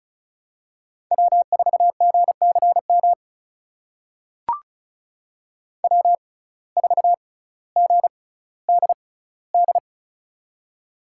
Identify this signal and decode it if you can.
Morse code, heard as a keyed tone, about 36 wpm, spaced as for 18 wpm, 700 Hz. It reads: W4GCM W4GDD
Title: USA Call Signs - 5 Characters - 6x Character Spacing - 35wpm